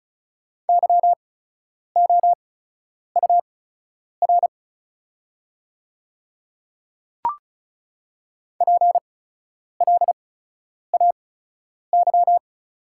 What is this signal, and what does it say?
Morse code, heard as a keyed tone, about 34 wpm, spaced as for 10 wpm, 690 Hz. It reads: YOUR PLAY
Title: 8x Character Spacing - Top 300 Words - 35wpm